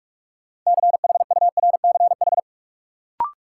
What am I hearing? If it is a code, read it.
Morse code, heard as a keyed tone, about 46 wpm, 700 Hz. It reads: CHURCH
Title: Sets of 5 Words - New Word Drill - 45wpm